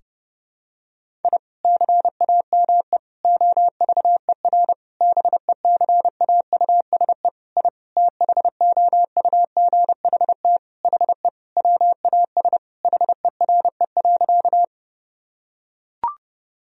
Morse code, heard as a keyed tone, about 30 wpm, 700 Hz. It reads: I CAME OVER BECAUSE I THOUGHT HE WAS HERE.
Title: Sentences from Top 1000 Words - Review Part 3 - 30wpm